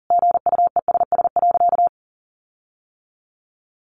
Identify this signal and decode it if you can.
Morse code, heard as a keyed tone, about 40 wpm, 700 Hz. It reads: GUESS.
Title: Sentences from Top 1500 Words 40wpm